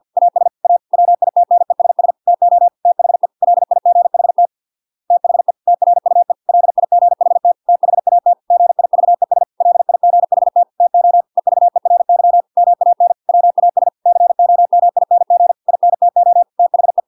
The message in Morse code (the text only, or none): AS A WITNESS TO THE LIGHT THE TRUE LIGHT THAT GIVES LIGHT TO EVERY MAN WAS COMING INTO THE